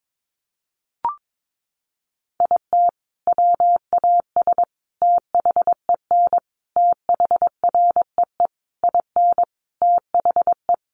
Morse code, heard as a keyed tone, about 22 wpm, 700 Hz. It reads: IT WAS THEN THREE IN THE
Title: Sentences from Top 1500 Words 22wpm